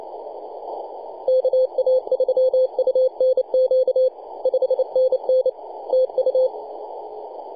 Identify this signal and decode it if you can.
Morse code, heard as a keyed tone, about 29 wpm, 520 Hz, with radio noise. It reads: KA3UNQ 5NN TU
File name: posts / 2015-june-21-0848z-yb4ir-9-14-010mhz-cw